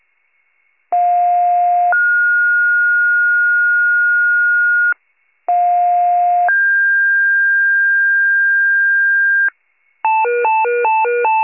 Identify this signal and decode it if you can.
Two-tone page, heard as a one-second tone, then a three-second tone, about 710 then 1430 Hz, about 710 then 1600 Hz, with noise. It is heard from 0.9 to 9.5 s.